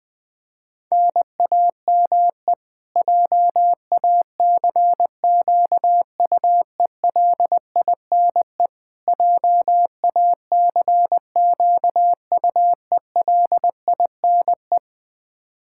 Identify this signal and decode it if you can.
Morse code, heard as a keyed tone, about 20 wpm, 700 Hz. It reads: NAME JACQUELINE JACQUELINE
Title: Practice QSO - Exchange 1 - Beginning - 20wpm